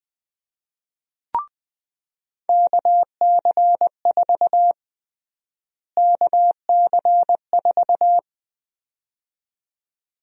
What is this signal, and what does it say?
Morse code, heard as a keyed tone, about 21 wpm, 700 Hz. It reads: KC4 KC4
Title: USA Call Sign Prefixes - Repeated 2x - 20wpm